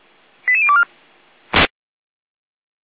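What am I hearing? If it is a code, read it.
Five-tone selective call (ZVEI), heard as quick runs of five tones, tones of about 70 ms. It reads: 88024
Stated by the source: ZVEI five-tone call received by SDR, fire brigade feuerwehr-orlamuende.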